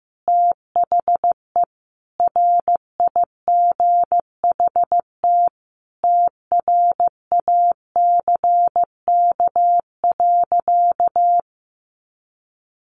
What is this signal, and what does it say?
Morse code, heard as a keyed tone, about 15 wpm, 700 Hz. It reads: THE RIGHT TRACK.
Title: Sentences from Top 1100 Words 15wpm